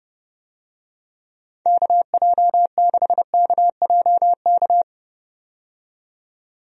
Morse code, heard as a keyed tone, about 30 wpm, 700 Hz. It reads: KJ6KJK
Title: USA Call Signs - 6 Characters - 30wpm